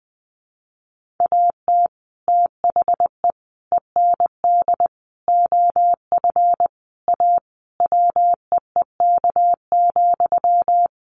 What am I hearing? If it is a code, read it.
Morse code, heard as a keyed tone, about 20 wpm, 700 Hz. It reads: AT THE END OF A WEEK,
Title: Sentences from Top 1400 Words 20wpm